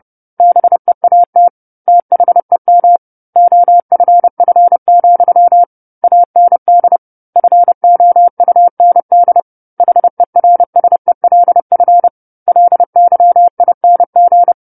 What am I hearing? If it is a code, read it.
Morse code, heard as a keyed tone, about 30 wpm, 700 Hz. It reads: BEAT THEM OFF, AND FOUND HERSELF LYING
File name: 10-Chapter-10 30 WPM